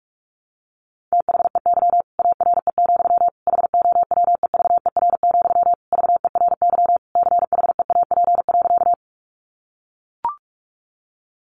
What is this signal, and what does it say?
Morse code, heard as a keyed tone, about 44 wpm, 700 Hz. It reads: THEY ARE, HOWEVER, VERY CHEAP.
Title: Sentences from Top 1200 Words 45wpm